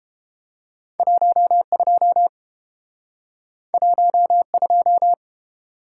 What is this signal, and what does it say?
Morse code, heard as a keyed tone, about 30 wpm, 700 Hz. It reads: 12 12